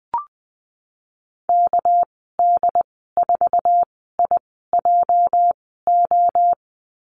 Morse code, heard as a keyed tone, about 20 wpm, spaced as for 15 wpm, 700 Hz. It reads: KD4IJO
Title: 219 Lesson 35g - Number 0 - Call Signs 20wpm